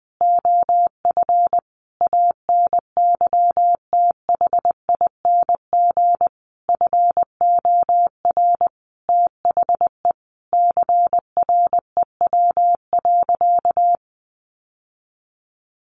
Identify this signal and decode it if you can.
Morse code, heard as a keyed tone, about 20 wpm, 700 Hz. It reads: OF ANYTHING FOR THE CREW.
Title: Sentences from Top 1500 Words 20wpm